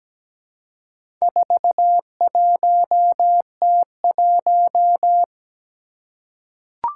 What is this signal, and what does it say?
Morse code, heard as a keyed tone, about 17 wpm, 700 Hz. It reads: 41T1
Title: Four Challenging Letter-Number Combinations 17wpm